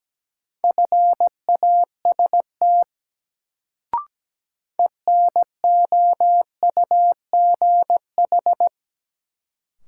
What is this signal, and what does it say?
Morse code, heard as a keyed tone, about 17 wpm, 700 Hz. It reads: FAST ENOUGH
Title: Sentences from Top 400 Words - New Word Drill - 17wpm